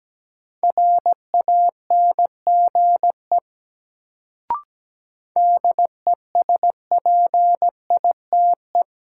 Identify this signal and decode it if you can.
Morse code, heard as a keyed tone, about 17 wpm, 700 Hz. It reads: RANGE DESPITE
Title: Sets of 4 Words - New Word Drill - 17wpm